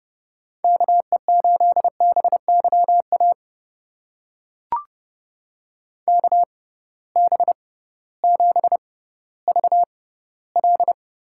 Morse code, heard as a keyed tone, about 31 wpm, spaced as for 16 wpm, 700 Hz. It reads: KE8BYA KB7VL
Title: USA Call Signs - 6 Characters - 6x Character Spacing - 30wpm